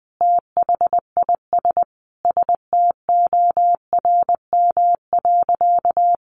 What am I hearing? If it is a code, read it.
Morse code, heard as a keyed tone, about 20 wpm, 700 Hz. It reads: THIS STORM.
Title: Sentences from Top 1500 Words 20wpm